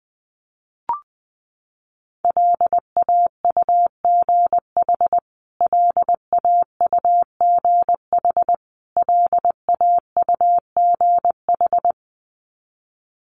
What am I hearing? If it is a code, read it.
Morse code, heard as a keyed tone, about 19 wpm, 700 Hz. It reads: LAUGH LAUGH LAUGH
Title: Top 500 Words Repeated 20wpm